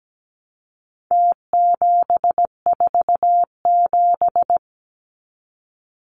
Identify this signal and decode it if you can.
Morse code, heard as a keyed tone, about 17 wpm, 700 Hz. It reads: T747
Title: Numbers - Contest Style - 4-Digit 17wpm